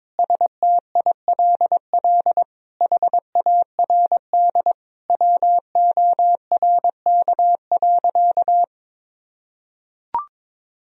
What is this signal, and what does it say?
Morse code, heard as a keyed tone, about 22 wpm, 700 Hz. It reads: STILL HARD WORK.
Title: Sentences from Top 1000 Words - Review Part 3 - 22wpm